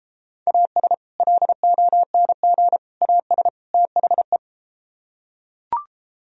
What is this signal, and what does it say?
Morse code, heard as a keyed tone, about 32 wpm, 700 Hz. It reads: AS LONG AS THE